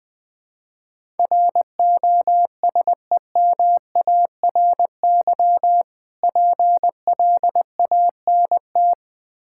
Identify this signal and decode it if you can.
Morse code, heard as a keyed tone, about 20 wpm, 700 Hz. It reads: ROSEMARY PLANT
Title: Difficult Words with Hints - 20wpm